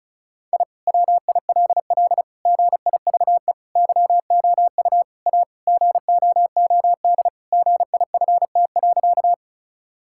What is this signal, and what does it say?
Morse code, heard as a keyed tone, about 36 wpm, 700 Hz. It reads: I WILL GIVE YOU A GOOD GIFT.
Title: Sentences from Top 1000 Words - Review Part 6 - 35wpm